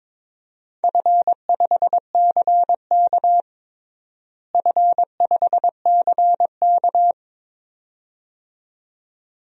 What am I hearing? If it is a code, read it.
Morse code, heard as a keyed tone, about 22 wpm, 700 Hz. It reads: F5CK F5CK
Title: DXCC-100 Call Signs - 4 Characters - Repeated 2x - 22wpm